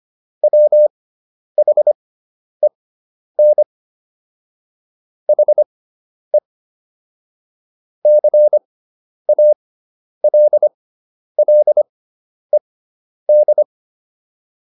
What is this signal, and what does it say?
Morse code, heard as a keyed tone, about 25 wpm, spaced as for 10 wpm, 600 Hz. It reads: WHEN HE CALLED